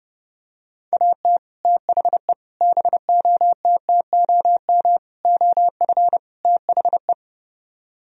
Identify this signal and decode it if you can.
Morse code, heard as a keyed tone, about 31 wpm, 700 Hz. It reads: AT THE BOTTOM OF THE